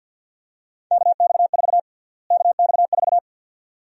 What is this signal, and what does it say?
Morse code, heard as a keyed tone, about 52 wpm, 700 Hz. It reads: KX4 KX4
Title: USA Call Sign Prefixes - Repeated 2x - 50wpm